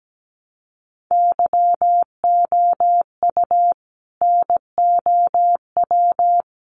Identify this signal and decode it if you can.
Morse code, heard as a keyed tone, about 17 wpm, 700 Hz. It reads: YOU NOW
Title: Sentences from Top 1100 Words 17wpm